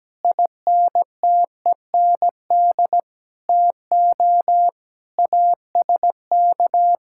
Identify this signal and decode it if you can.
Morse code, heard as a keyed tone, about 17 wpm, 700 Hz. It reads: INTEND TO ASK
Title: Sentences from Top 1600 Words 17wpm